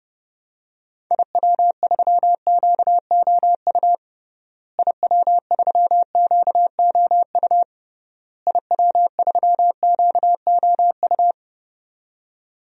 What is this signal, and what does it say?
Morse code, heard as a keyed tone, about 32 wpm, 700 Hz. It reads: IW3QOU IW3QOU IW3QOU
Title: DXCC-100 Call Signs - 6 Characters - Repeated 3x - 30wpm